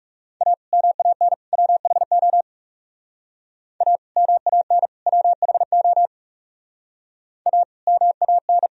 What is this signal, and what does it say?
Morse code, heard as a keyed tone, about 37 wpm, 700 Hz. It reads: A MAN WHO A MAN WHO A MAN
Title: Speed-Racing - Sets of 3 Words 30wpm